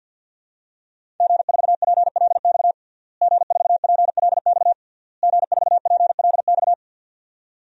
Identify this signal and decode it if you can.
Morse code, heard as a keyed tone, about 46 wpm, 700 Hz. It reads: G4PLX G4PLX G4PLX
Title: DXCC-100 Call Signs - 5 Characters - Repeated 3x - 50wpm